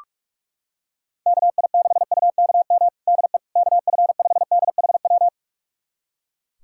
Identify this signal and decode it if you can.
Morse code, heard as a keyed tone, about 41 wpm, 700 Hz. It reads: KI6UKM DE KF5DHW
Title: USA Callsign DE Callsign - 6 Characters - 45wpm